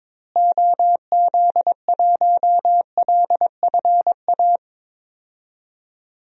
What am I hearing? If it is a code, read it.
Morse code, heard as a keyed tone, about 22 wpm, 700 Hz. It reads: OZ1LFA
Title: DXCC-100 Call Signs - 6 Characters - Repeated 3x - 22wpm